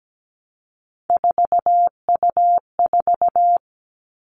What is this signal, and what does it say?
Morse code, heard as a keyed tone, about 17 wpm, 700 Hz. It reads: 4U4
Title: Three Challenging Letter-Number Combinations 17wpm